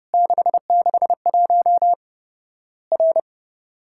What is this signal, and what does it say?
Morse code, heard as a keyed tone, about 32 wpm, 700 Hz. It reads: 661 R
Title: Worked All Europe DX Contest - QTC Exchange - 30wpm